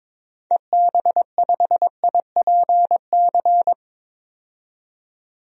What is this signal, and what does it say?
Morse code, heard as a keyed tone, about 22 wpm, 700 Hz. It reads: EB5IPC
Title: DXCC-100 Call Signs - 6 Characters - 22wpm